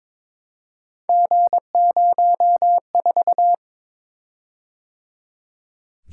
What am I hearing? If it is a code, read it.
Morse code, heard as a keyed tone, about 23 wpm, 700 Hz. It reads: G04
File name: practice / 22wpm / Three Letter-Number Combinations - 22wpm